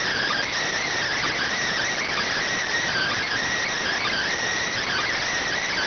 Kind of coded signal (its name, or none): Emergency Alert System (SAME) tones